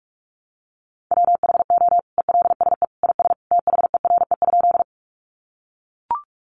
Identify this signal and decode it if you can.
Morse code, heard as a keyed tone, about 44 wpm, 700 Hz. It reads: WHO ELSE IS THERE?
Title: Sentences from Top 800 Words 45wpm